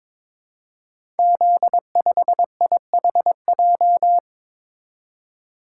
Morse code, heard as a keyed tone, about 22 wpm, 700 Hz. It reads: Z5IHJ
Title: Code Groups - 5 Letter:Numbers - 22wpm